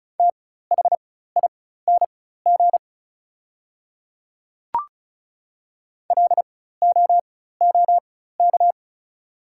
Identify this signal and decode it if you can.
Morse code, heard as a keyed tone, about 36 wpm, spaced as for 16 wpm, 700 Hz. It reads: THING LOOK
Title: 4x Character Spacing - Top 200 Words - 35wpm